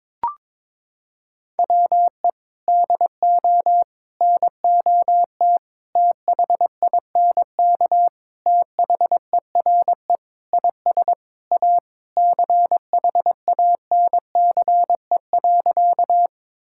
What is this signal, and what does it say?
Morse code, heard as a keyed tone, about 22 wpm, 700 Hz. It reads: WE DO NOT THINK THERE IS A CHANCE.
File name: practice / 22wpm / Sentences with 4-Word Phrases - 22wpm